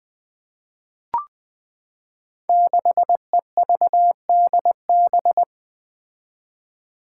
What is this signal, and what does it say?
Morse code, heard as a keyed tone, about 20 wpm, 700 Hz. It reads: 6EVDB
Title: Five Challenging Letter-Number Combinations 20wpm